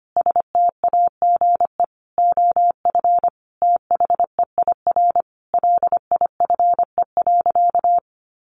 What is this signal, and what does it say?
Morse code, heard as a keyed tone, about 25 wpm, 700 Hz. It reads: STAGE OF THEIR LIFE.
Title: Sentences from Top 1200 Words 25wpm